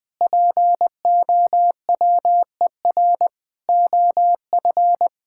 Morse code, heard as a keyed tone, about 21 wpm, 700 Hz. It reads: POWER OF
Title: Sentences with 4-Word Phrases - 20wpm